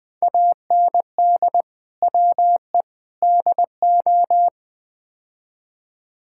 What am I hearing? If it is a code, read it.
Morse code, heard as a keyed tone, about 20 wpm, 700 Hz. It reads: AND WE DO
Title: Build A Phrase - 4-Word Phrases - 20wpm